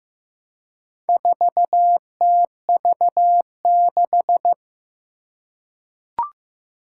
Morse code, heard as a keyed tone, about 15 wpm, 700 Hz. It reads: 4TV6
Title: Four Challenging Letter-Number Combinations 15wpm